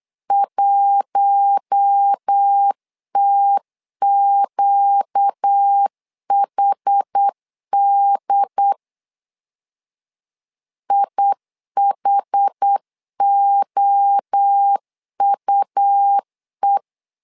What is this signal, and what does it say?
Morse code, heard as a keyed tone, about 8 wpm, 800 Hz. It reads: 1TQHD IHOUE
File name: LEZIONI / 4 Q1 8wpm 3035